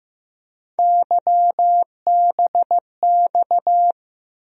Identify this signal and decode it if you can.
Morse code, heard as a keyed tone, about 15 wpm, 700 Hz. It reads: YBX
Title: USA Call Sign Postfixes - 3 Letters 15wpm